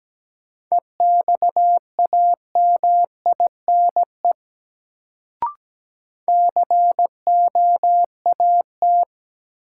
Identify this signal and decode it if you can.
Morse code, heard as a keyed tone, about 17 wpm, 700 Hz. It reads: EXAMINE COAT
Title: Sentences from Top 1600 Words - New Word Drill - 17wpm